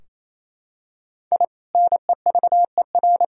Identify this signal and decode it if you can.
Morse code, heard as a keyed tone, about 28 wpm, 700 Hz. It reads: I NEVER
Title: Sentences from Top 1000 Words - Review Part 3 - 28wpm